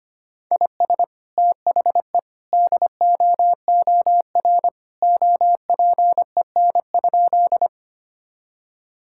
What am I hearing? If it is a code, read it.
Morse code, heard as a keyed tone, about 25 wpm, 700 Hz. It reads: IS THE DOOR OPEN?
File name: practice / 25wpm / Sentences with 2-Word Phrases - 25wpm